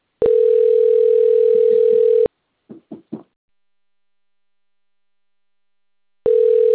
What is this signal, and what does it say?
Ringback tone (North America), 440 + 480 Hz.